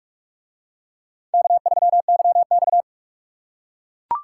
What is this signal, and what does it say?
Morse code, heard as a keyed tone, about 42 wpm, 700 Hz. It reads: K3YX